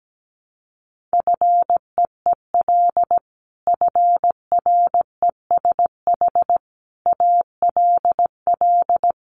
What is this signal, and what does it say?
Morse code, heard as a keyed tone, about 17 wpm, 700 Hz. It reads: FEEL FRESH ALL
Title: Sentences from Top 1200 Words 17wpm